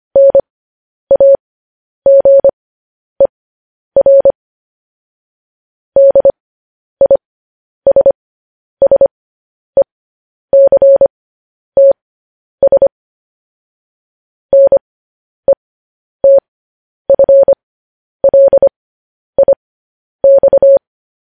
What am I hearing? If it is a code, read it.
Morse code, heard as a keyed tone, about 25 wpm, spaced as for 10 wpm, 550 Hz. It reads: NAGER DISSECTS NETFLIX
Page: news-headlines-in-morse-code-at-10-wpm